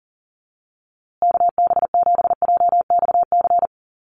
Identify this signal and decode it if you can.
Morse code, heard as a keyed tone, about 40 wpm, 700 Hz. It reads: KB7JXC
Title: USA Call Signs Repeated 3x 40wpm